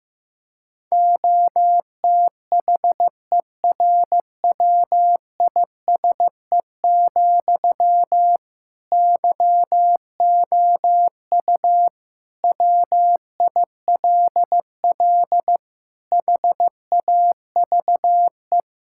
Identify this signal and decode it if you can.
Morse code, heard as a keyed tone, about 15 wpm, 700 Hz. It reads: OTHERWISE, YOU WILL HAVE
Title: Sentences from Top 1600 Words 15wpm